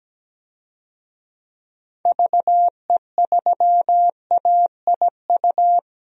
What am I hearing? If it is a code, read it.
Morse code, heard as a keyed tone, about 17 wpm, 700 Hz. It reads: VE3AIU
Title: DXCC-100 Call Signs - 6 Characters - Repeated 2x - 17wpm